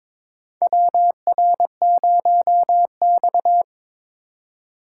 Morse code, heard as a keyed tone, about 21 wpm, 700 Hz. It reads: WR0X
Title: USA Call Signs - 4 Characters - 4x Character Spacing - 22wpm